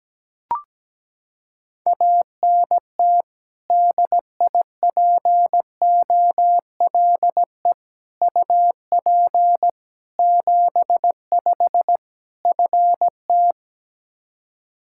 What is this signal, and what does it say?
Morse code, heard as a keyed tone, about 17 wpm, 700 Hz. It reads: ANT DIPOLE UP 75 FT
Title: K1USN SST QSO Practice 17wpm